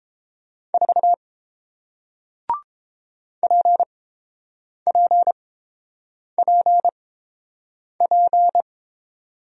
Morse code, heard as a keyed tone, about 31 wpm, spaced as for 7 wpm, 700 Hz. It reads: 4 PPPP